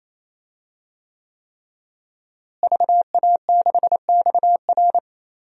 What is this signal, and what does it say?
Morse code, heard as a keyed tone, about 27 wpm, 700 Hz. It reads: VA6XR
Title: DXCC-100 Call Signs - 5 Characters - Repeated 3x - 28wpm